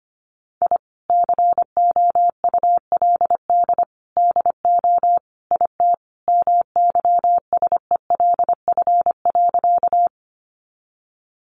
Morse code, heard as a keyed tone, about 25 wpm, 700 Hz. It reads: I COULD DO IT MYSELF.
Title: Sentences from Top 1300 Words 25wpm